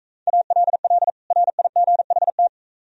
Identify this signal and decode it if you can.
Morse code, heard as a keyed tone, about 44 wpm, 700 Hz. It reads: ALL RIGHT